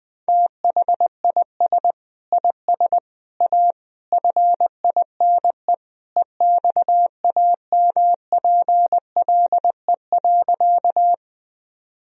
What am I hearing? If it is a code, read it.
Morse code, heard as a keyed tone, about 20 wpm, 700 Hz. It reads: THIS IS A FINE EXAMPLE.
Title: Sentences from Top 1000 Words - Review Part 1 - 20wpm